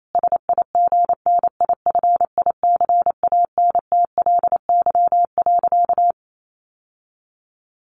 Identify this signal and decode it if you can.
Morse code, heard as a keyed tone, about 28 wpm, 700 Hz. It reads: SIGNIFICANTLY.
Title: Sentences from Top 1500 Words 28wpm